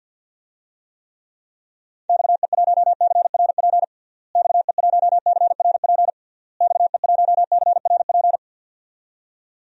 Morse code, heard as a keyed tone, about 49 wpm, 700 Hz. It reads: XE1CRP XE1CRP XE1CRP
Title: DXCC-100 Call Signs - 6 Characters - Repeated 3x - 50wpm